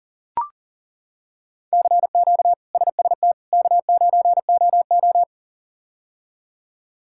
Morse code, heard as a keyed tone, about 37 wpm, 700 Hz. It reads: CQ SST K9OO
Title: K1USN SST QSO Practice - 40wpm